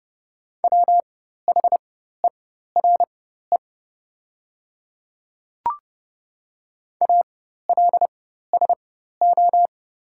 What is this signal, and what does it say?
Morse code, heard as a keyed tone, about 31 wpm, spaced as for 14 wpm, 700 Hz. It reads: WHERE ALSO
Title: 4x Character Spacing - Top 200 Words - 30wpm